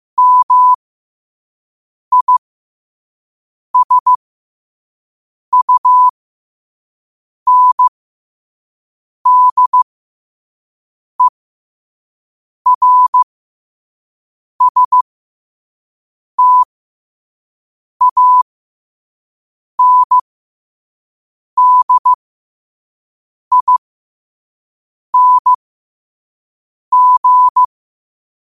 Morse code, heard as a keyed tone, about 15 wpm, spaced as for 5 wpm, 1000 Hz. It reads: MISUNDERSTANDING
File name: qotdfiles / 5wpm